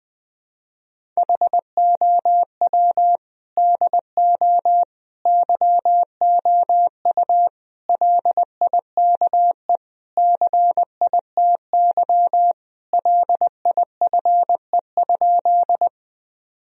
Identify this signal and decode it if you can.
Morse code, heard as a keyed tone, about 20 wpm, 700 Hz. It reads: HOW DO YOU LIKE CITY LIFE?
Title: Sentences from Top 1000 Words - Review Part 1 - 20wpm